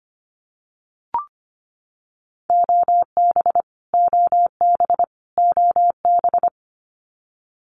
Morse code, heard as a keyed tone, about 25 wpm, 700 Hz. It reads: OB OB OB
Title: Abbreviations Repeated 3x - 25wpm